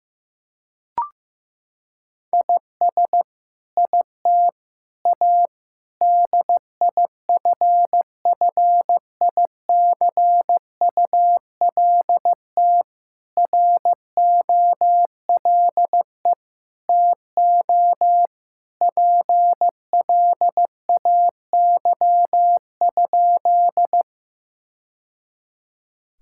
Morse code, heard as a keyed tone, about 16 wpm, 700 Hz. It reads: IS IT A DIFFICULT ROLE TO PLAY?
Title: Sentences from Top 1000 Words - Review Part 6 - 15wpm